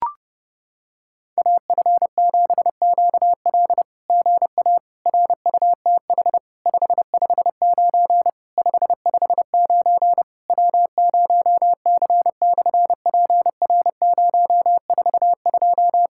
Morse code, heard as a keyed tone, about 30 wpm, 700 Hz. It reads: AF7QL GA RUTH 559 559 W0C/PR042
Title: SOTA QSO Practice 30wpm